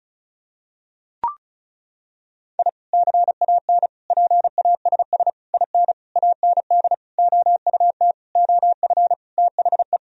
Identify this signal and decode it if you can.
Morse code, heard as a keyed tone, about 34 wpm, 700 Hz. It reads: I CAN PASS IN AND OUT OF THE
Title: Sentences from Top 1000 Words - Review Part 4 - 35wpm